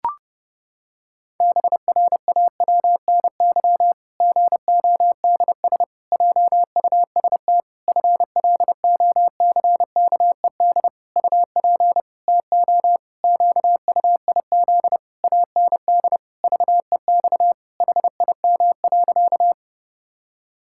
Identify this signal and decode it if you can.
Morse code, heard as a keyed tone, about 30 wpm, 700 Hz. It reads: BRAWNY GODS JUST FLOCKED UP TO QUIZ AND VEX HIM.
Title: Pangram Challenge v2 - 30wpm